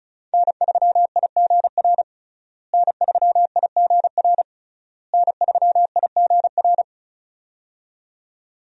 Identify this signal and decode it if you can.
Morse code, heard as a keyed tone, about 35 wpm, 700 Hz. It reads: N3IGR N3IGR N3IGR
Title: USA Call Signs - 5 Characters - Repeated 3x - 35wpm